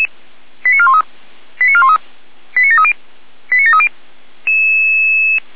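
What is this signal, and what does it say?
Five-tone selective call (ZVEI), heard as quick runs of five tones, tones of about 70 ms. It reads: 89413 89413 89730 89730
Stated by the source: ZVEI five-tone selective call received by SDR.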